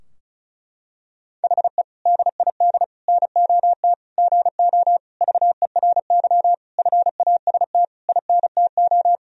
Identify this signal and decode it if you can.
Morse code, heard as a keyed tone, about 34 wpm, 700 Hz. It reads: HE DID NOT GO VERY FAST INTO
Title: Sentences from Top 1000 Words - Review Part 2 - 35wpm